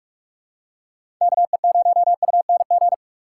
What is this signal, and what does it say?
Morse code, heard as a keyed tone, about 47 wpm, 700 Hz. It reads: KE0UNG
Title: USA Call Signs - 6 Characters - 4x Character Spacing - 45wpm